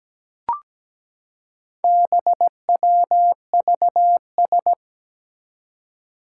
Morse code, heard as a keyed tone, about 17 wpm, 700 Hz. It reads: BWVS
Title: Four Challenging Letter-Number Combinations 17wpm